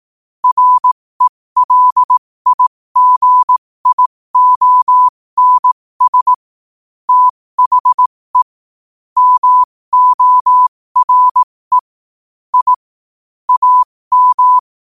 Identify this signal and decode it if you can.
Morse code, heard as a keyed tone, about 18 wpm, spaced as for 15 wpm, 1000 Hz. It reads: RELIGIONS THE MORE I AM
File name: qotdfiles / 15wpm